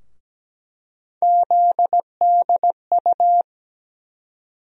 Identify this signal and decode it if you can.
Morse code, heard as a keyed tone, about 17 wpm, 700 Hz. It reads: ZDU
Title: USA Call Sign Postfixes - 3 Letters 17wpm